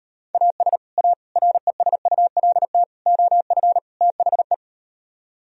Morse code, heard as a keyed tone, about 38 wpm, 700 Hz. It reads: AS A RESULT OF THE